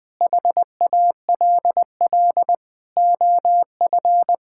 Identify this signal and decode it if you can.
Morse code, heard as a keyed tone, about 20 wpm, 700 Hz. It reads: HALL OF
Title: Sentences with 6-Word Phrases - Spoken Beginning - 20wpm